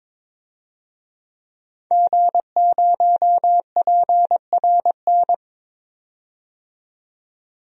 Morse code, heard as a keyed tone, about 22 wpm, 700 Hz. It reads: G0PRN
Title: DXCC-100 Call Signs - 5 Characters - 22wpm